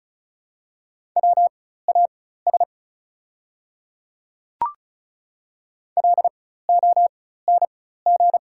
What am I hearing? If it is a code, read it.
Morse code, heard as a keyed tone, about 34 wpm, spaced as for 16 wpm, 700 Hz. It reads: WAS LONG
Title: 4x Character Spacing - Top 200 Words - 35wpm